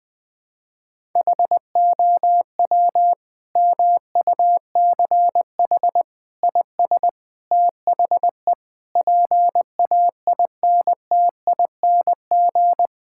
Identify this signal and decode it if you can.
Morse code, heard as a keyed tone, about 20 wpm, 700 Hz. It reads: HOW MUCH IS THE PAINTING
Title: Sentences from Top 1000 Words - Review Part 6 - 20wpm